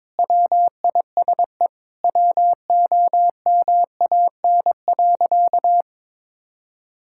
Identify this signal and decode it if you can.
Morse code, heard as a keyed tone, about 22 wpm, 700 Hz. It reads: WISE WOMAN.
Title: Sentences from Top 1600 Words 22wpm